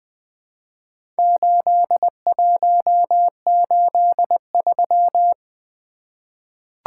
Morse code, heard as a keyed tone, about 20 wpm, 700 Hz. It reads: 8183
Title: Four Letter-Number Combinations - 20wpm